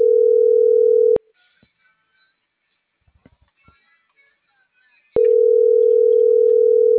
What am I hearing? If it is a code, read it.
Ringback tone (North America), 440 + 480 Hz.